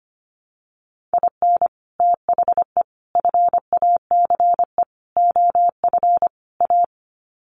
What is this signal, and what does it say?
Morse code, heard as a keyed tone, about 25 wpm, 700 Hz. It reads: IN THE FACE OF A
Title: Sets of 5 Words Encore 25wpm